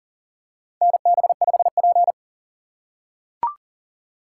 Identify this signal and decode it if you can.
Morse code, heard as a keyed tone, about 40 wpm, 700 Hz. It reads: NB5P
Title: USA Call Signs - 4 Characters - 2x Character Spacing - 40wpm